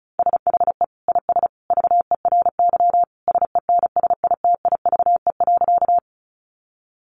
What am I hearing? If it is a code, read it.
Morse code, heard as a keyed tone, about 34 wpm, 700 Hz. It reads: SHE IS VERY SENSITIVE.
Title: Sentences from Top 1400 Words 35wpm